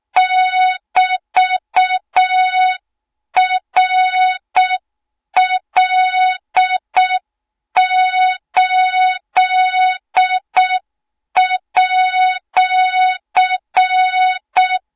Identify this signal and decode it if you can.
Morse code, heard as a keyed tone, about 6 wpm, 3700 Hz. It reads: <BT>RL8@
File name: Morze-HA2MN / hang / 20 GYAKORLAT